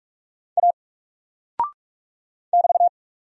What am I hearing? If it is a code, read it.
Morse code, heard as a keyed tone, about 45 wpm, 700 Hz. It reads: A <BT>
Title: Single Character (Letter:Number:Punctuation:Prosign) Alt 45wpm